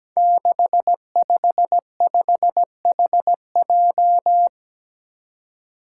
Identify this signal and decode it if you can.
Morse code, heard as a keyed tone, about 17 wpm, 700 Hz. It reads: 655HJ
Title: Five Challenging Letter-Number Combinations 17wpm